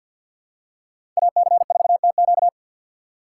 Morse code, heard as a keyed tone, about 53 wpm, 700 Hz. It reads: AC4TQ
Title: USA Call Signs - 5 Characters - 6x Character Spacing - 50wpm